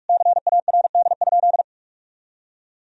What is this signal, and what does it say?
Morse code, heard as a keyed tone, about 46 wpm, 700 Hz. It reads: CARD?
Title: Sentences from Top 1000 Words - Review Part 4 - 45wpm